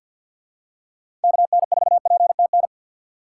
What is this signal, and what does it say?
Morse code, heard as a keyed tone, about 52 wpm, 700 Hz. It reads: KN4PTN
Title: USA Call Signs - 6 Characters - 6x Character Spacing - 50wpm